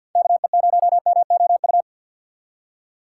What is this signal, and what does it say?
Morse code, heard as a keyed tone, about 53 wpm, 700 Hz. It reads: KE0MOU
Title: USA Call Signs - 4x Character Spacing - 50wpm